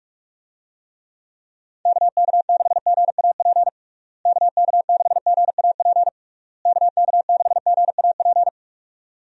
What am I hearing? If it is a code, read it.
Morse code, heard as a keyed tone, about 41 wpm, 700 Hz. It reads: KK6GAP KK6GAP KK6GAP
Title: USA Call Signs - 6 Characters - Repeated 3x - 45wpm